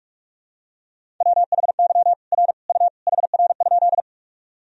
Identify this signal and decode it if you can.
Morse code, heard as a keyed tone, about 47 wpm, 700 Hz. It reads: WHY R U HR?
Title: Sentences from Top 200 Words and Abbreviations 45wpm